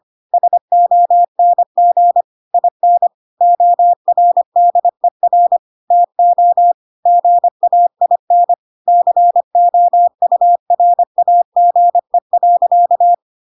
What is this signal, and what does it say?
Morse code, heard as a keyed tone, about 25 wpm, 700 Hz. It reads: SONG IN ORDER TO GAIN COURAGE.
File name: Pinocchio / Pinocchio ch1 25 25 ik1hge